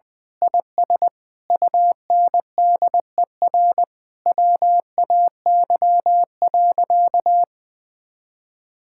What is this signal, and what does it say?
Morse code, heard as a keyed tone, about 21 wpm, 700 Hz. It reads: IS UNDER WAY.
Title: Sentences with 9-Word Phrases - 20wpm